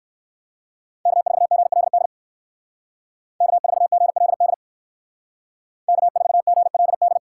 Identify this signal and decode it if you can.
Morse code, heard as a keyed tone, about 54 wpm, 700 Hz. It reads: K4GLD K4GLD K4GLD
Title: Speed Racing - USA 1x3 Call Signs 45wpm